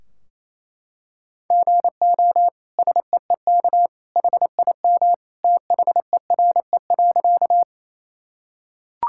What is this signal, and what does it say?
Morse code, heard as a keyed tone, about 28 wpm, 700 Hz. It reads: GO SEEK HIM THERE.
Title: Sentences from Top 1600 Words 28wpm